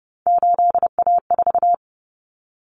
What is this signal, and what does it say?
Morse code, heard as a keyed tone, about 30 wpm, 700 Hz. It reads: 8A4
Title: Common Cut Numbers - 3-Digit 30wpm